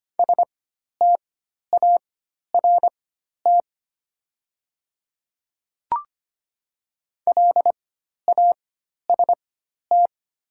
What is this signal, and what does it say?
Morse code, heard as a keyed tone, about 25 wpm, spaced as for 12 wpm, 690 Hz. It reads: START LAST
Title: 4x Character Spacing - Top 300 Words - 25wpm